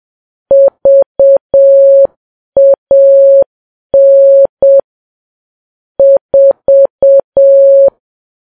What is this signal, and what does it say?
Morse code, heard as a keyed tone, about 7 wpm, 550 Hz. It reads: VAN 4
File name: Files Web tls / Morse on9fv / morse7kps